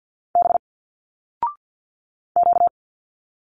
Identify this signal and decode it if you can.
Morse code, heard as a keyed tone, about 46 wpm, 700 Hz. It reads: B Q
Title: Single Letters 50wpm